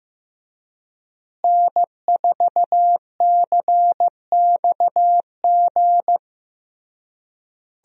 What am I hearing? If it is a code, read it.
Morse code, heard as a keyed tone, about 15 wpm, 700 Hz. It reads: N4CXG
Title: USA Call Signs - 5 Characters - Repeated 3x - 15wpm